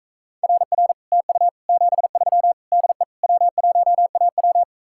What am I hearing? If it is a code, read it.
Morse code, heard as a keyed tone, about 45 wpm, 700 Hz. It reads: RR TU 73 DE W1AW